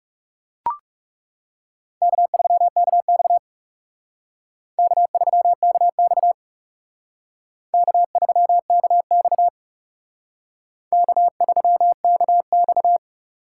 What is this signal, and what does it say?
Morse code, heard as a keyed tone, about 37 wpm, 700 Hz. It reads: K3KX K3KX K3KX K3KX
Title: Speed Racing - USA 1x2 Call Signs 30wpm